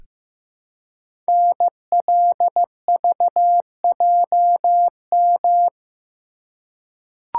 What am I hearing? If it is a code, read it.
Morse code, heard as a keyed tone, about 15 wpm, 700 Hz. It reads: NLVJM
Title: Code Groups - 5 Letters - 15wpm